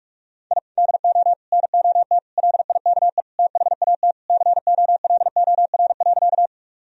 Morse code, heard as a keyed tone, about 46 wpm, 700 Hz. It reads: I DO NOT LIKE THAT COLOR.
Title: Sentences from Top 1000 Words - Review Part 2 - 45wpm